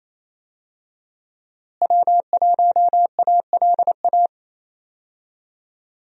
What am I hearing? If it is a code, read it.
Morse code, heard as a keyed tone, about 28 wpm, 700 Hz. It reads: W1ALA
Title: USA Call Signs - 5 Characters - Repeated 3x - 28wpm